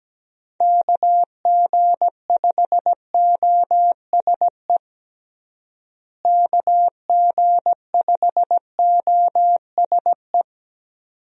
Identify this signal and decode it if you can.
Morse code, heard as a keyed tone, about 17 wpm, 700 Hz. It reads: KG5OSE KG5OSE
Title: USA Call Signs - 6 Characters - Repeated 2x - 17wpm